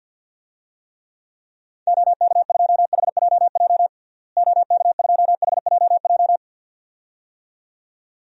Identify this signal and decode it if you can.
Morse code, heard as a keyed tone, about 51 wpm, 700 Hz. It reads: OK2HJJ OK2HJJ
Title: DXCC-100 Call Signs - 6 Characters - Repeated 2x - 50wpm